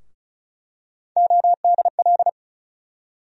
Morse code, heard as a keyed tone, about 36 wpm, 700 Hz. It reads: ODL
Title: USA Call Sign Postfixes - 3 Letters 35wpm